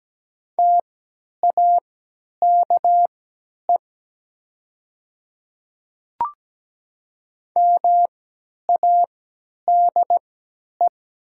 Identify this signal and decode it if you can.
Morse code, heard as a keyed tone, about 17 wpm, spaced as for 10 wpm, 700 Hz. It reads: TAKE MADE
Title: 3x Character Spacing - Top 200 Words - 17wpm